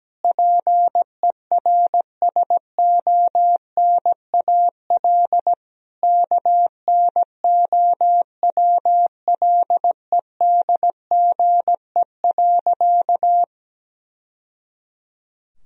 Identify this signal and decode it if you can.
Morse code, heard as a keyed tone, about 17 wpm, 700 Hz. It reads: PERSONAL KNOWLEDGE.
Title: Sentences from Top 1000 Words - Review Part 4 - 17wpm